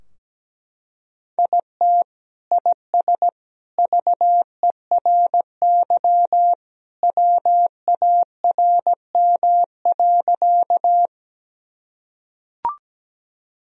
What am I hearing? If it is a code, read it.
Morse code, heard as a keyed tone, about 17 wpm, 700 Hz. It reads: IT IS VERY WARM.
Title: Sentences from Top 1000 Words - Review Part 2 - 17wpm